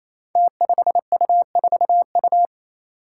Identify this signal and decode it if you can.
Morse code, heard as a keyed tone, about 28 wpm, 700 Hz. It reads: T5U4U
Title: Five Challenging Letter-Number Combinations 28wpm